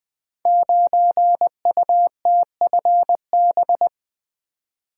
Morse code, heard as a keyed tone, about 20 wpm, 700 Hz. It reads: 9UTFB
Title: Code Groups - 5 Letter:Numbers - 20wpm